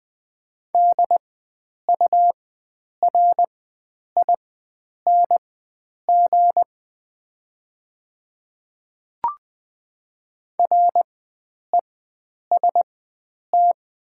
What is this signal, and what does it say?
Morse code, heard as a keyed tone, about 20 wpm, spaced as for 9 wpm, 700 Hz. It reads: DURING REST
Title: 4x Character Spacing - Top 500 Words - 20wpm